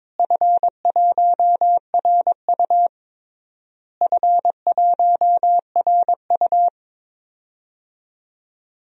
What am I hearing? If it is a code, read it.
Morse code, heard as a keyed tone, about 22 wpm, 700 Hz. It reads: F1RU F1RU
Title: DXCC-100 Call Signs - 4 Characters - Repeated 3x - 22wpm